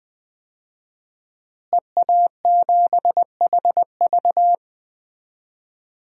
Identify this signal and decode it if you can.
Morse code, heard as a keyed tone, about 20 wpm, 700 Hz. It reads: EA7HV
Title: DXCC-100 Call Signs - 5 Characters - Repeated 2x - 20wpm